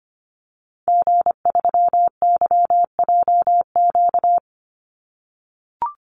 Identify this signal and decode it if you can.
Morse code, heard as a keyed tone, about 25 wpm, 700 Hz. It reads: G3YJQ
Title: Call Signs - 25wpm